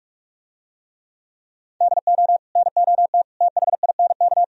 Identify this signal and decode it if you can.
Morse code, heard as a keyed tone, about 47 wpm, 700 Hz. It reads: DO NOT THINK